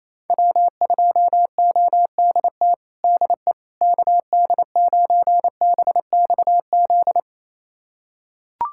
Morse code, heard as a keyed tone, about 28 wpm, 700 Hz. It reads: W2ODT DE KD9BXZ
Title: USA Callsign DE Callsign 28wpm